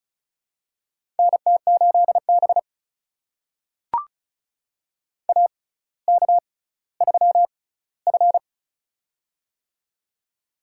Morse code, heard as a keyed tone, about 36 wpm, spaced as for 18 wpm, 700 Hz. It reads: NT8B AK3F
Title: USA Call Signs - 6x Character Spacing - 35wpm